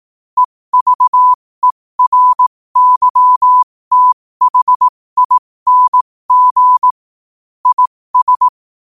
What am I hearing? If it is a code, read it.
Morse code, heard as a keyed tone, about 18 wpm, spaced as for 15 wpm, 1000 Hz. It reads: EVERYTHING IS
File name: qotdfiles / 15wpm